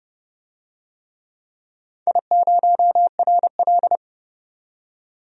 Morse code, heard as a keyed tone, about 30 wpm, 700 Hz. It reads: I0RL
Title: DXCC-100 Call Signs - 4 Characters - 30wpm